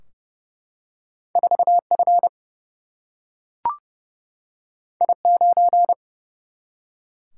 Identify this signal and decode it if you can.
Morse code, heard as a keyed tone, about 30 wpm, 700 Hz. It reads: 4F I9
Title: Single-Letter Single-Number 30wpm